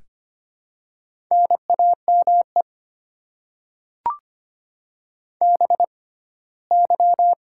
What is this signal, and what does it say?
Morse code, heard as a keyed tone, about 25 wpm, spaced as for 17 wpm, 690 Hz. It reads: NAME BY
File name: Top 90 QSO Elements - 6x Character Spacing - 25wpm